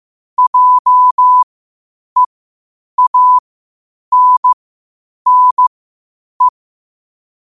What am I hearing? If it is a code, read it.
Morse code, heard as a keyed tone, about 15 wpm, spaced as for 8 wpm, 1000 Hz. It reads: JEANNE
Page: Quotes for Sat, 16 Aug 2025 in Morse Code at 8 words per minute.